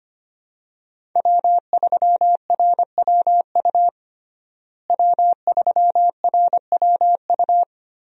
Morse code, heard as a keyed tone, about 25 wpm, 700 Hz. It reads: W3RWU W3RWU
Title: USA Call Signs - 5 Characters - Repeated 3x - 25wpm